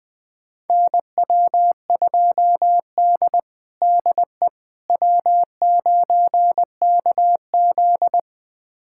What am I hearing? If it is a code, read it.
Morse code, heard as a keyed tone, about 20 wpm, 700 Hz. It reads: NW2D DE W9KZ
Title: USA Callsign DE Callsign - 4 Characters - 20wpm